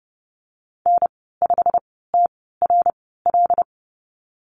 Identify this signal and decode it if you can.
Morse code, heard as a keyed tone, about 28 wpm, spaced as for 17 wpm, 700 Hz. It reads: N5TRL
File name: Lesson 15g - Number 5 - Call Signs 30wpm